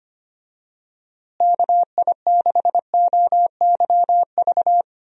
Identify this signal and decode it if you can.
Morse code, heard as a keyed tone, about 25 wpm, 700 Hz. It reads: KI6OYV
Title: USA Call Signs - 6 Characters - Repeated 2x - 25wpm